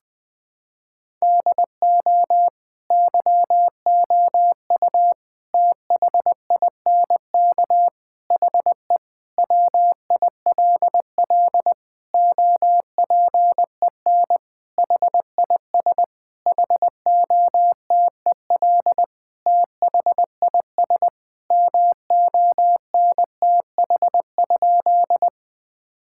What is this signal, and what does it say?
Morse code, heard as a keyed tone, about 20 wpm, 700 Hz. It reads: DO YOU THINK HE WILL OPEN HIS HOTEL THIS MONTH?
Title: Sentences from Top 1600 Words 20wpm